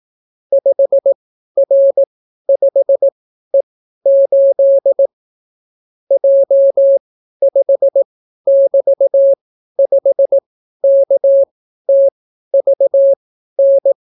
Morse code, heard as a keyed tone, about 18 wpm, spaced as for 12 wpm, 550 Hz. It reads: 5R5E8 J5<BT>5KTVN